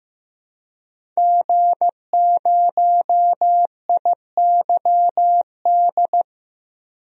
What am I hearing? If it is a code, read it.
Morse code, heard as a keyed tone, about 15 wpm, 700 Hz. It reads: G0IYD
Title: DXCC-100 Call Signs - 5 Characters - Repeated 3x - 15wpm